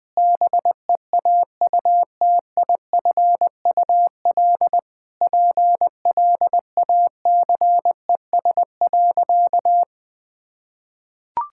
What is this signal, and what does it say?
Morse code, heard as a keyed tone, about 20 wpm, 700 Hz. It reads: BEAUTIFUL PLACES.
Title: Sentences with 3-Word Phrases - Spoken Beginning - 20wpm